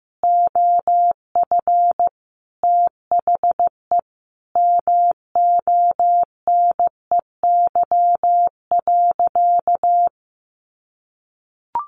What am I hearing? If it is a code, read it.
Morse code, heard as a keyed tone, about 15 wpm, 700 Hz. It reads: OF THE MONEY.
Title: Sentences from Top 1500 Words 15wpm